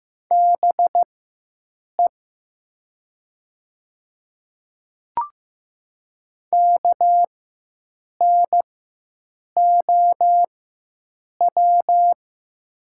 Morse code, heard as a keyed tone, about 15 wpm, spaced as for 7 wpm, 700 Hz. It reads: BE KNOW
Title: 4x Character Spacing - Top 100 Words - 15wpm